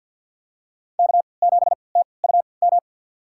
Morse code, heard as a keyed tone, about 51 wpm, spaced as for 29 wpm, 700 Hz. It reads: K7TUM
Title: USA Call Signs - 3x Character Spacing - 50wpm